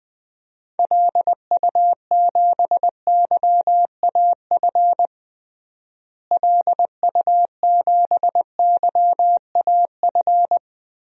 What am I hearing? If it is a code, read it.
Morse code, heard as a keyed tone, about 20 wpm, 700 Hz. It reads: LU7YAF LU7YAF
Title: DXCC-100 Call Signs - 6 Characters - Repeated 3x - 20wpm